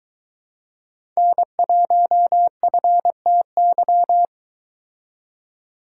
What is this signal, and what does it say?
Morse code, heard as a keyed tone, about 24 wpm, 700 Hz. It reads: N1FTY